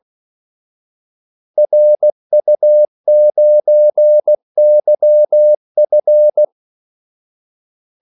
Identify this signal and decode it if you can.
Morse code, heard as a keyed tone, about 16 wpm, 600 Hz. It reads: RU9YF
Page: Kaldesignaler 16 wpm